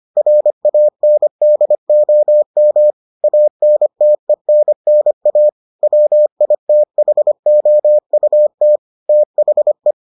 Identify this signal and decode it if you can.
Morse code, heard as a keyed tone, about 25 wpm, 600 Hz. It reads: RANDOM ANTENNA WITHOUT THE